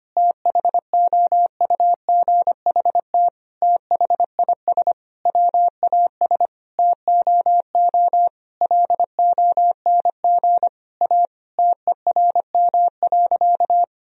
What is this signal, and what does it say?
Morse code, heard as a keyed tone, about 25 wpm, 700 Hz. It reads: THOUGHT THIS WAS TOO LONG A TERM.
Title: Sentences from Top 1000 Words - Review Part 4 - 25wpm